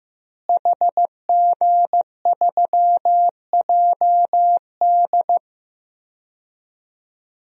Morse code, heard as a keyed tone, about 15 wpm, 700 Hz. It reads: HG3JD
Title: DXCC-100 Call Signs - 5 Characters - Repeated 3x - 15wpm